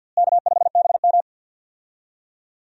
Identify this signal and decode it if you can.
Morse code, heard as a keyed tone, about 50 wpm, 700 Hz. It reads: K5BM
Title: USA Call Signs - 4 Characters - Repeated 3x - 50wpm